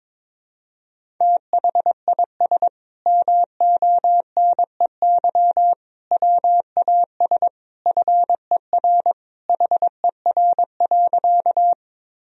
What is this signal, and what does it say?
Morse code, heard as a keyed tone, about 22 wpm, 700 Hz. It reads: THIS MONEY WAS FER HER.
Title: Sentences from Top 400 Words and Abbreviations 22wpm